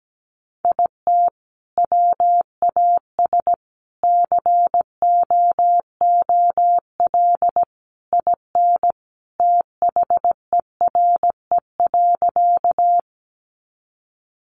Morse code, heard as a keyed tone, about 17 wpm, 700 Hz. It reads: IT WAS COOL IN THERE.
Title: Sentences from Top 1200 Words 17wpm